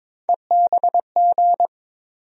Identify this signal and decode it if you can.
Morse code, heard as a keyed tone, about 21 wpm, 700 Hz. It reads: EBG
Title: USA Call Sign Postfixes - 3 Letters 22wpm